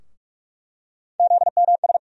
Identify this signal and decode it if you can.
Morse code, heard as a keyed tone, about 46 wpm, 700 Hz. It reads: ZMS
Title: USA Call Sign Postfixes - 3 Letters 45wpm